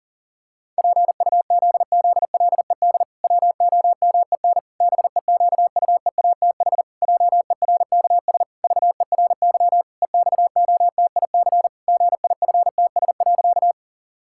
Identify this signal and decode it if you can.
Morse code, heard as a keyed tone, about 40 wpm, 700 Hz. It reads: PUZZLED WOMEN BEQUEATH JERKS VERY EXOTIC GIFTS.
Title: Pangram Challenge v2 - 40wpm